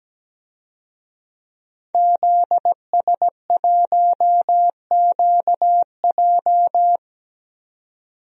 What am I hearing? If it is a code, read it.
Morse code, heard as a keyed tone, about 17 wpm, 700 Hz. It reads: ZS1QJ
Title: DXCC-100 Call Signs - 5 Characters - 17wpm